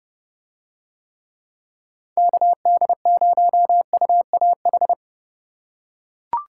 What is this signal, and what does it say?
Morse code, heard as a keyed tone, about 28 wpm, 700 Hz. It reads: KD0UAH